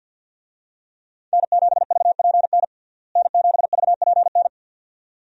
Morse code, heard as a keyed tone, about 52 wpm, 700 Hz. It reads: N7VPN N7VPN
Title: USA Call Signs - 5 Characters - Repeated 2x - 50wpm